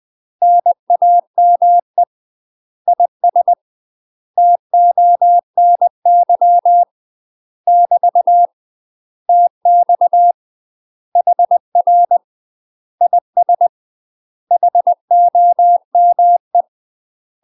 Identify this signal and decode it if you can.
Morse code, heard as a keyed tone, about 20 wpm, 700 Hz. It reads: NAME IS TONY <BT> TX HR IS HOME
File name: UK MASTER QSOs TRAINING / 20WPM / UK MASTER QSOs 0071